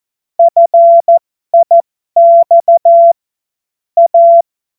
Morse code, heard as a keyed tone, about 14 wpm, spaced as for 12 wpm, 680 Hz. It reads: FIX A
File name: Morsekurs / Witze